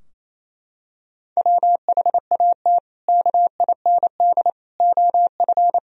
Morse code, heard as a keyed tone, about 28 wpm, 700 Hz. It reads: WHAT KIND OF
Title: Sentences from Top 1000 Words - Review Part 4 - 28wpm